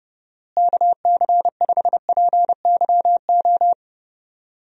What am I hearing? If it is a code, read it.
Morse code, heard as a keyed tone, about 28 wpm, 700 Hz. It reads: KC5PYO
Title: USA Call Signs - 6 Characters - Repeated 3x - 30wpm